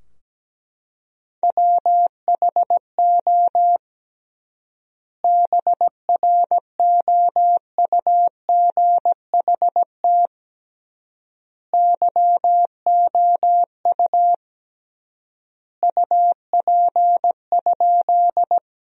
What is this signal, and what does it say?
Morse code, heard as a keyed tone, about 17 wpm, 700 Hz. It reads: WHO BROUGHT YOU UP?
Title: 3x Word-Spacing - Sentences from Top 500 Words - 17wpm